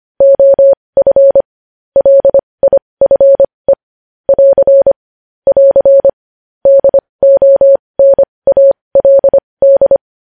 Morse code, heard as a keyed tone, about 25 wpm, spaced as for 20 wpm, 550 Hz. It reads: OF LIFE <AR> <AR> DONALD
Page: news headlines in morse code at 20 wpm